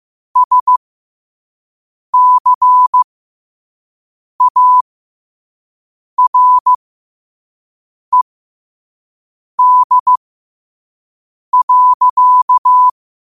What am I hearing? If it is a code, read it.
Morse code, heard as a keyed tone, about 15 wpm, spaced as for 5 wpm, 1000 Hz. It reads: SCARED.
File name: 5wpm